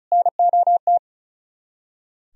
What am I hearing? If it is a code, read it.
Morse code, heard as a keyed tone, about 43 wpm, 700 Hz. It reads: NOT
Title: Speed-Racing - Sets of 4 Words 35wpm